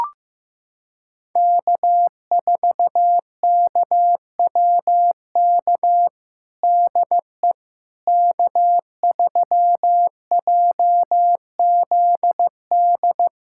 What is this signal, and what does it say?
Morse code, heard as a keyed tone, about 15 wpm, 700 Hz. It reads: K4KWK DE K3JZD
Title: USA Callsign DE Callsign - 5 Characters - 15wpm